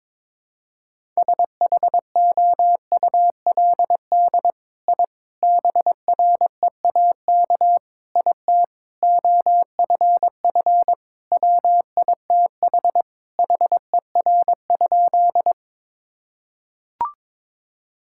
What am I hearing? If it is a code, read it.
Morse code, heard as a keyed tone, about 22 wpm, 700 Hz. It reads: SHOULD I BREAK IT OFF WITH HER?
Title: Sentences from Top 1000 Words - Review Part 6 - 22wpm